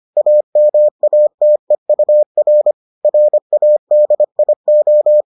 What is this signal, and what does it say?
Morse code, heard as a keyed tone, about 25 wpm, 600 Hz. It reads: AMATEUR RADIO